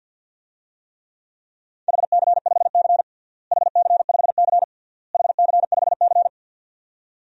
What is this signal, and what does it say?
Morse code, heard as a keyed tone, about 52 wpm, 700 Hz. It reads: HC5C HC5C HC5C
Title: DXCC-100 Call Signs - 4 Characters - Repeated 3x - 50wpm